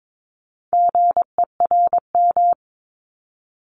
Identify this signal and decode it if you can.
Morse code, heard as a keyed tone, about 22 wpm, 700 Hz. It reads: GERM
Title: Multi-letter Phonemes Repeated 3x 22wpm